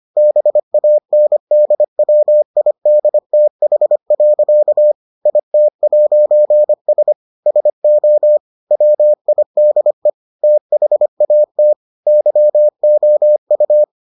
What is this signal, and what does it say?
Morse code, heard as a keyed tone, about 25 wpm, 600 Hz. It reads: BANDWIDTH. IT'S SO WIDE THAT YOU